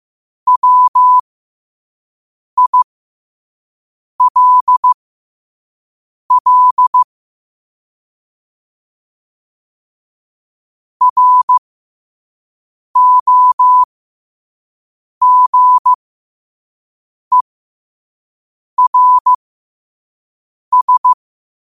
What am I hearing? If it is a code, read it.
Morse code, heard as a keyed tone, about 15 wpm, spaced as for 5 wpm, 1000 Hz. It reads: WILL ROGERS
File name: qotdfiles / 5wpm